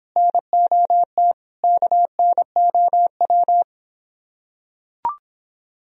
Morse code, heard as a keyed tone, about 26 wpm, 700 Hz. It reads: NOT KNOW